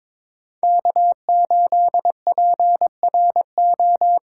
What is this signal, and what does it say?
Morse code, heard as a keyed tone, about 22 wpm, 700 Hz. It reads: K8PRO
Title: USA Callsign DE Callsign - 5 Characters - 22wpm